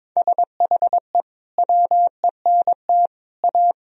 Morse code, heard as a keyed tone, about 22 wpm, 700 Hz. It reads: SHE WENT A
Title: Sentences from Top 1000 Words - Review Part 2 - 22wpm